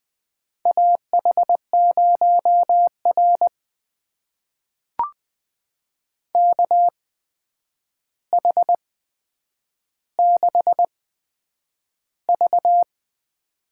Morse code, heard as a keyed tone, about 20 wpm, spaced as for 9 wpm, 700 Hz. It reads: AH0R KH6V
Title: USA Call Signs - 8x Character Spacing - 20wpm